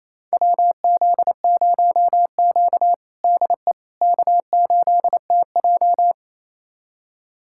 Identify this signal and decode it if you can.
Morse code, heard as a keyed tone, about 28 wpm, 700 Hz. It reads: WZ0Q DE K8TJ
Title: USA Callsign DE Callsign - 4 Characters - 28wpm